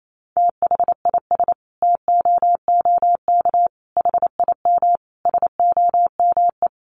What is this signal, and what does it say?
Morse code, heard as a keyed tone, about 28 wpm, 700 Hz. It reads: THIS TOOK HIM SOME
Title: Sentences from Top 300 Words 28wpm